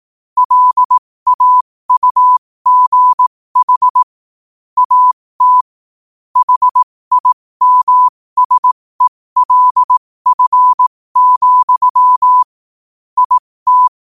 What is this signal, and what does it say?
Morse code, heard as a keyed tone, about 18 wpm, spaced as for 15 wpm, 1000 Hz. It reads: LAUGH AT HIMSELF, IT
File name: qotdfiles / 15wpm